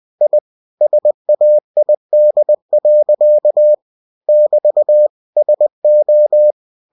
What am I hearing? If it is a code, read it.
Morse code, heard as a keyed tone, about 20 wpm, 600 Hz. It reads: I SAID. <BT> SO